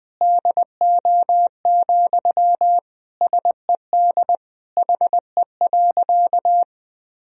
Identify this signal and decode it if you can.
Morse code, heard as a keyed tone, about 20 wpm, 700 Hz. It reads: DO, SED HE.
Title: Sentences from Top 100 Words and Abbreviations 20wpm